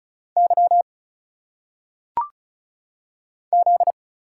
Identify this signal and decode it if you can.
Morse code, heard as a keyed tone, about 35 wpm, 700 Hz. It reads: Y Z
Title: Single Letters - Phonetic Pronunciation 35wpm